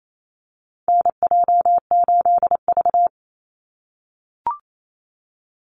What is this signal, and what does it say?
Morse code, heard as a keyed tone, about 28 wpm, 700 Hz. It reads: NJ8V
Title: Call Signs - 28wpm